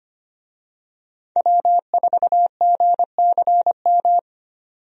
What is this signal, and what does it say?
Morse code, heard as a keyed tone, about 25 wpm, 690 Hz. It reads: W4GCM
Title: USA Call Signs - 5 Characters - 25wpm